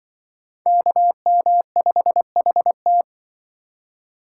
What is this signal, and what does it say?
Morse code, heard as a keyed tone, about 24 wpm, 700 Hz. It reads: KM5HT
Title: Speed Racing - USA 2x2 Call Signs 20wpm